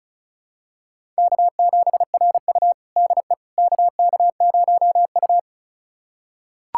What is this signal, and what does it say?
Morse code, heard as a keyed tone, about 34 wpm, 700 Hz. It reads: K7RU DE KK0U
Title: USA Callsign DE Callsign - 4 Characters - 35wpm